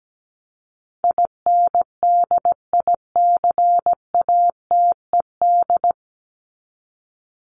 Morse code, heard as a keyed tone, about 17 wpm, 700 Hz. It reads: INDICATED
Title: ED Suffix - Top 100 Words with Base Words 17wpm